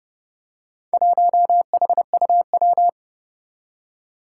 Morse code, heard as a keyed tone, about 30 wpm, 700 Hz. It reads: 1HUW
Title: Four Challenging Letter-Number Combinations 30wpm